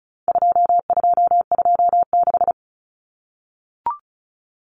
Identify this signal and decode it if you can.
Morse code, heard as a keyed tone, about 34 wpm, 700 Hz. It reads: 2226
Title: Common Cut Numbers - 4-Digit 35wpm